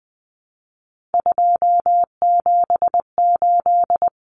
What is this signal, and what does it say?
Morse code, heard as a keyed tone, about 20 wpm, 700 Hz. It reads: 278
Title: Common Cut Numbers - 3-Digit 20wpm